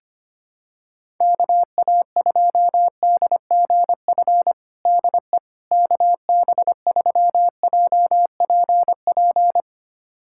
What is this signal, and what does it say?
Morse code, heard as a keyed tone, about 25 wpm, 700 Hz. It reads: KA2DGF DE KB3JPP
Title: USA Callsign DE Callsign - 6 Characters - 25wpm